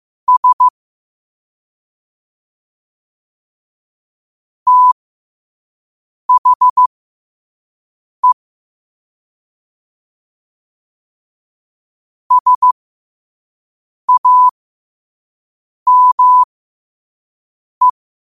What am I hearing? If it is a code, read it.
Morse code, heard as a keyed tone, about 15 wpm, spaced as for 5 wpm, 1000 Hz. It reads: S THE SAME
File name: qotdfiles / 5wpm